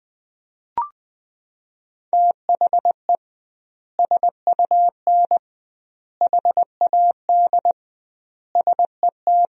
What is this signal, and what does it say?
Morse code, heard as a keyed tone, about 20 wpm, 700 Hz. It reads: THE SUN HAD SET
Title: 2x Word-Spacing - Sentences from Top 300 Words - 20wpm